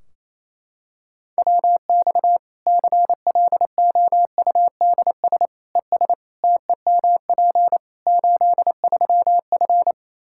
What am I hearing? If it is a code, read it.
Morse code, heard as a keyed tone, about 28 wpm, 700 Hz. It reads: WX CLOUDS ES TEMP 83F
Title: K1USN SST QSO Practice 28wpm